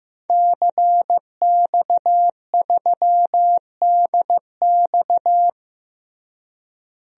Morse code, heard as a keyed tone, about 15 wpm, 700 Hz. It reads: CX3DX
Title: DXCC-100 Call Signs - 5 Characters - Repeated 3x - 15wpm